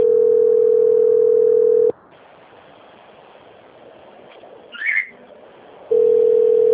Ringback tone (North America), 440 + 480 Hz.
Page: Toledo - PR